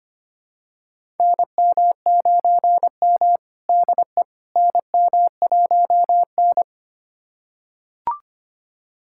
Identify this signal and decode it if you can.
Morse code, heard as a keyed tone, about 25 wpm, 700 Hz. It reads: NM9M DE NM1N
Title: USA Callsign DE Callsign - 4 Characters - 25wpm